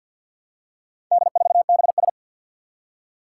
Morse code, heard as a keyed tone, about 50 wpm, 700 Hz. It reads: D4BS
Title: Four Challenging Letter-Number Combinations 50wpm